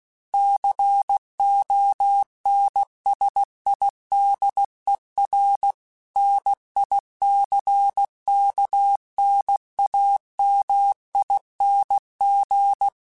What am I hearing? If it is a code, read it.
Morse code, heard as a keyed tone, about 16 wpm, 800 Hz. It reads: CONSIDER NICKNAMING
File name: Technical Subjects / Technical Subjects-mp3s / OR Gate @15